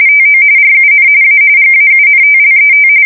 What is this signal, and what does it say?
RTTY, about 45.45 baud. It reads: TO RUN A GAUNTLET
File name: fsk170 / USS-New-Jersey